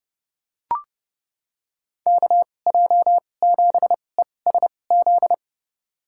Morse code, heard as a keyed tone, about 30 wpm, spaced as for 22 wpm, 700 Hz. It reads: KJ7ESZ
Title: USA Call Signs - 6 Characters - 2x Character Spacing - 30wpm